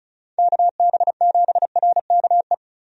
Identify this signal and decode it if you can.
Morse code, heard as a keyed tone, about 35 wpm, 700 Hz. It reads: KB7RKE
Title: USA Call Signs - 6 Characters - 35wpm